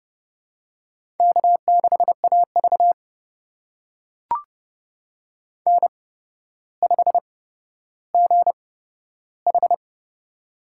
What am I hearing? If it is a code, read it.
Morse code, heard as a keyed tone, about 32 wpm, spaced as for 13 wpm, 700 Hz. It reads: K6AV N5GH
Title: USA Call Signs - 4 Characters - 8x Character Spacing - 30wpm